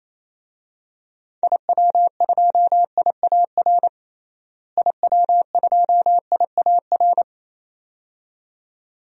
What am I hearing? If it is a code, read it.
Morse code, heard as a keyed tone, about 28 wpm, 700 Hz. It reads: IW2IAR IW2IAR
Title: DXCC-100 Call Signs - 6 Characters - Repeated 2x - 28wpm